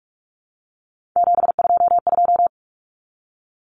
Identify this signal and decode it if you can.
Morse code, heard as a keyed tone, about 44 wpm, 700 Hz. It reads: 722
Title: Numbers - 3-Digit 45wpm